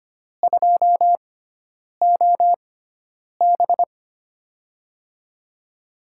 Morse code, heard as a keyed tone, about 25 wpm, spaced as for 9 wpm, 700 Hz. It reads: 2OB
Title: USA Call Signs - 6x Character Spacing - 25wpm